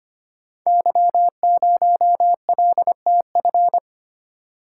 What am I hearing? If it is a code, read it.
Morse code, heard as a keyed tone, about 26 wpm, 700 Hz. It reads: Y0LTF
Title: Code Groups - 5 Letter:Numbers - 25wpm